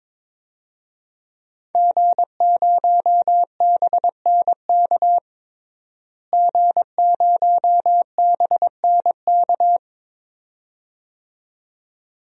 Morse code, heard as a keyed tone, about 22 wpm, 700 Hz. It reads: G0BNK G0BNK
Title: DXCC-100 Call Signs - 5 Characters - Repeated 2x - 22wpm